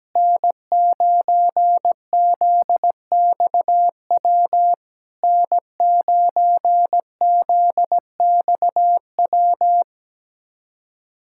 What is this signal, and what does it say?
Morse code, heard as a keyed tone, about 17 wpm, 700 Hz. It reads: N9ZXW N9ZXW
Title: SOTA QSO Practice 17wpm